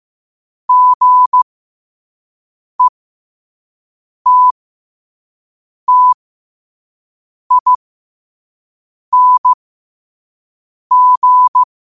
Morse code, heard as a keyed tone, about 15 wpm, spaced as for 5 wpm, 1000 Hz. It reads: GETTING
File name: qotdfiles / 5wpm